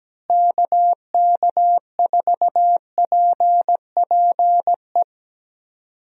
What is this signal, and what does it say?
Morse code, heard as a keyed tone, about 17 wpm, 700 Hz. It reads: KK4PPE
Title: USA Call Signs - 6 Characters - Repeated 3x - 17wpm